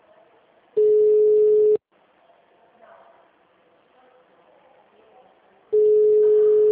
Ringback tone (Europe, 425 Hz).
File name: out- -2012- - - .45448